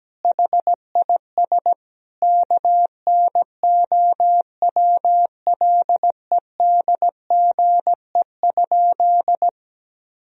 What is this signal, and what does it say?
Morse code, heard as a keyed tone, about 17 wpm, 700 Hz. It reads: HIS KNOWLEDGE?
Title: Sentences from Top 1000 Words - Review Part 3 - 17wpm